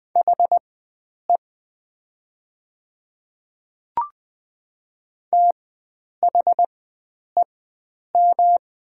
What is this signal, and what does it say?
Morse code, heard as a keyed tone, about 20 wpm, spaced as for 9 wpm, 700 Hz. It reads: HE THEM
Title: 4x Character Spacing - Top 100 Words - 20wpm